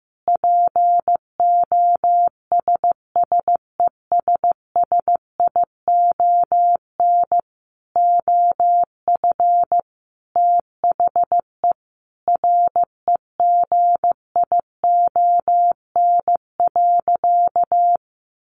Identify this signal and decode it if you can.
Morse code, heard as a keyed tone, about 15 wpm, 700 Hz. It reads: POSSESSION OF THE REGION.
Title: Sentences from Top 1500 Words 15wpm